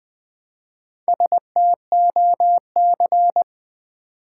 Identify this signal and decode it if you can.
Morse code, heard as a keyed tone, about 20 wpm, 700 Hz. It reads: STOC
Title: Build A Word - 1 Letter At A Time - 5-Letter Words - 20wpm